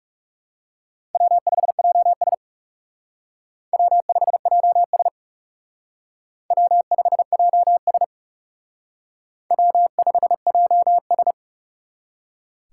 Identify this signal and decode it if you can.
Morse code, heard as a keyed tone, about 37 wpm, 700 Hz. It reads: W5JS W5JS W5JS W5JS
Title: Speed Racing - USA 1x2 Call Signs 30wpm